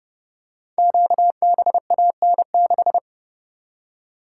Morse code, heard as a keyed tone, about 30 wpm, 700 Hz. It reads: QBAN6
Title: Code Groups - 5 Letter:Numbers - 30wpm